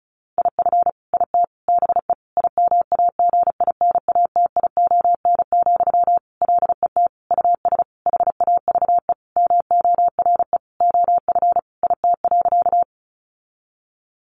Morse code, heard as a keyed tone, about 34 wpm, 700 Hz. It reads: IF IT BE IMAGINATION, LET US HAVE MORE OF IT.
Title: Sentences from Top 1300 Words 35wpm